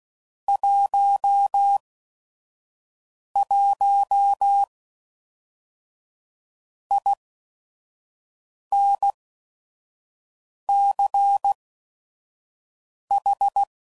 Morse code, heard as a keyed tone, about 16 wpm, spaced as for 5 wpm, 800 Hz. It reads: 11 INCH
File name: CW Code Groups @5WPM / QSOs / QSO WA7QOX 1